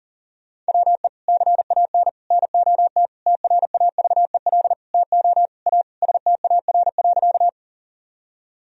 Morse code, heard as a keyed tone, about 40 wpm, 700 Hz. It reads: WE CAN NOT TRAVEL TO A STAR.
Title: Sentences from Top 1000 Words - Review Part 2 - 40wpm